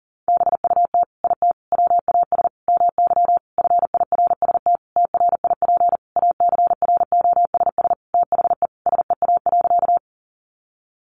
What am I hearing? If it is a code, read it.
Morse code, heard as a keyed tone, about 40 wpm, 700 Hz. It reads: BUT IT WAS MY FIRST TRIP ACROSS THE SEA.
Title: Sentences from Top 1300 Words 40wpm